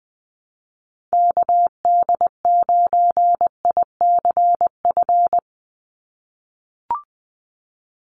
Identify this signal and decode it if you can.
Morse code, heard as a keyed tone, about 20 wpm, 700 Hz. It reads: KD9ICF
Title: USA Call Signs 20wpm